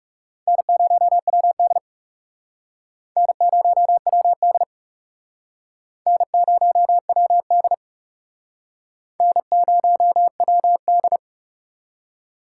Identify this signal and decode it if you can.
Morse code, heard as a keyed tone, about 37 wpm, 700 Hz. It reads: N0WD N0WD N0WD N0WD
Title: Speed Racing - USA Call Signs 30wpm